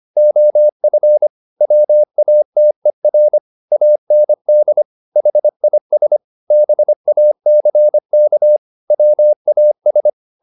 Morse code, heard as a keyed tone, about 25 wpm, 600 Hz. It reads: OF WATER AND HIS BACK WAS